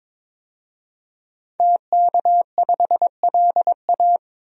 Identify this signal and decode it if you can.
Morse code, heard as a keyed tone, about 21 wpm, 700 Hz. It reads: TK5LA
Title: DXCC-100 Call Signs - 5 Characters - 22wpm